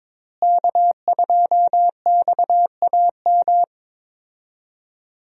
Morse code, heard as a keyed tone, about 22 wpm, 700 Hz. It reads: K2XAM
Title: USA Call Signs - 5 Characters - Repeated 3x - 22wpm